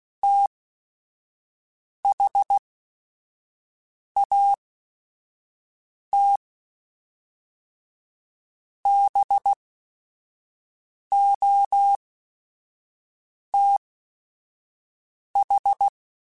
Morse code, heard as a keyed tone, about 16 wpm, spaced as for 5 wpm, 800 Hz. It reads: THAT BOTH
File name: Technical Subjects / Technical Subjects-mp3s / Reactance @5